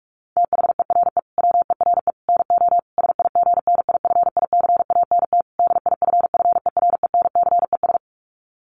Morse code, heard as a keyed tone, about 44 wpm, 700 Hz. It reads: THERE WERE NO SIGNIFICANT DIFFERENCES
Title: Sets of 5 Words Encore2 45wpm